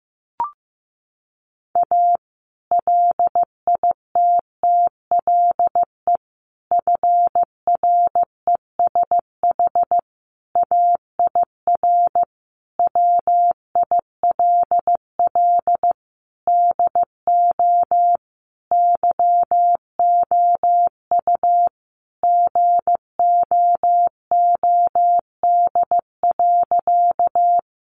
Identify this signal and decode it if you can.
Morse code, heard as a keyed tone, about 15 wpm, 700 Hz. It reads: A LITTLE FRESH AIR WILL DO YOU GOOD.
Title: Sentences from Top 1200 Words 15wpm